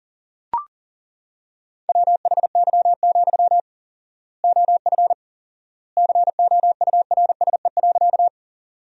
Morse code, heard as a keyed tone, about 40 wpm, 700 Hz. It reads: WHY, OF COURSE.
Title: 4x Word-Spacing - Sentences from Top 500 Words - 40wpm